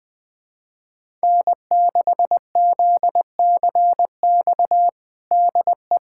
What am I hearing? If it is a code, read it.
Morse code, heard as a keyed tone, about 20 wpm, 700 Hz. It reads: N6ZCX DE
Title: USA Callsign DE Callsign - 5 Characters - 20wpm